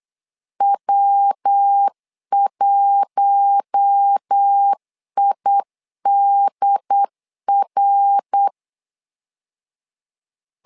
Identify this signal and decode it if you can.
Morse code, heard as a keyed tone, about 8 wpm, 800 Hz. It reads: W1IDR
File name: LEZIONI / 4 Q1 8wpm 3035